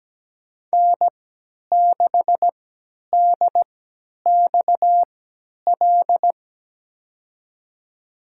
Morse code, heard as a keyed tone, about 17 wpm, spaced as for 10 wpm, 700 Hz. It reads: N6DXL
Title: USA Call Signs - 5 Characters - 3x Character Spacing - 17wpm